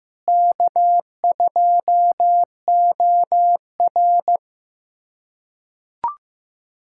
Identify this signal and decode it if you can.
Morse code, heard as a keyed tone, about 15 wpm, 700 Hz. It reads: K2OR
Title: USA Callsign DE Callsign 15wpm